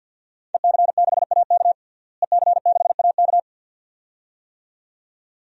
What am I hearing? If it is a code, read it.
Morse code, heard as a keyed tone, about 47 wpm, 700 Hz. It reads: EC6AK EC6AK
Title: DXCC-100 Call Signs - 5 Characters - Repeated 2x - 50wpm